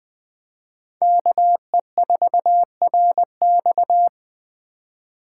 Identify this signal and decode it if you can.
Morse code, heard as a keyed tone, about 21 wpm, 700 Hz. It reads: KE4RX